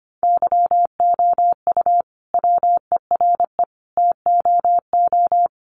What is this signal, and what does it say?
Morse code, heard as a keyed tone, about 25 wpm, 700 Hz. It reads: YOU WERE TOO
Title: Sentences from Top 1500 Words 25wpm